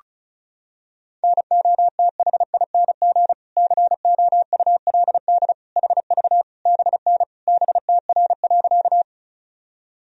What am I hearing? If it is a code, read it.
Morse code, heard as a keyed tone, about 36 wpm, 700 Hz. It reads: NOTHING COULD HV BN BTR.
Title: Sentences from Top 500 Words and Abbreviations 35wpm